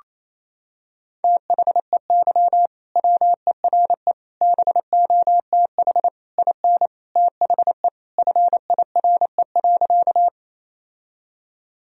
Morse code, heard as a keyed tone, about 27 wpm, 700 Hz. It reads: THEY WERE BOTH IN THE FIRE.
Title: Sentences from Top 1000 Words - Review Part 1 - 28wpm